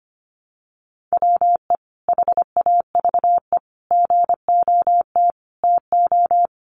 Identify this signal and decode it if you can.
Morse code, heard as a keyed tone, about 25 wpm, 700 Hz. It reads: WE HAVE GOT TO
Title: Sentences from Top 1200 Words 25wpm